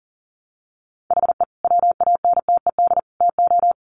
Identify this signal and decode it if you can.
Morse code, heard as a keyed tone, about 40 wpm, 700 Hz. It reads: HE WANTED TO
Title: Sets of 3 Words 40wpm